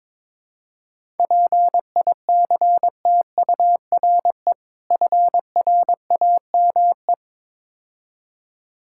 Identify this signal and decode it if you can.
Morse code, heard as a keyed tone, about 22 wpm, 690 Hz. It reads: PICTURE FRAME
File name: Easy Words - Speed-Racing with Hints - 22wpm